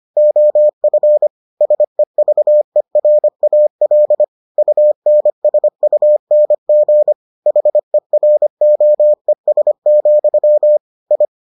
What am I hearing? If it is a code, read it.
Morse code, heard as a keyed tone, about 25 wpm, 600 Hz. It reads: OF SEVERAL UNSUNG HEROES, I